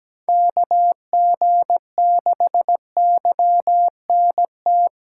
Morse code, heard as a keyed tone, about 20 wpm, 700 Hz. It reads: KG6YNT
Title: Speed Racing - USA Call Signs 17wpm